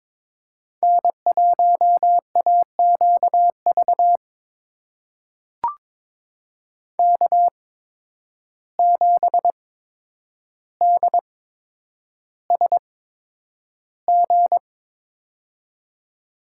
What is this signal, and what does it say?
Morse code, heard as a keyed tone, about 22 wpm, spaced as for 9 wpm, 700 Hz. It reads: N1AQV K7DSG
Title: USA Call Signs - 5 Characters - 8x Character Spacing - 22wpm